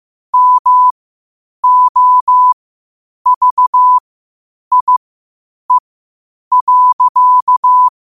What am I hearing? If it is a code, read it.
Morse code, heard as a keyed tone, about 15 wpm, spaced as for 8 wpm, 1000 Hz. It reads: MOVIE.
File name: qotdfiles / 8wpm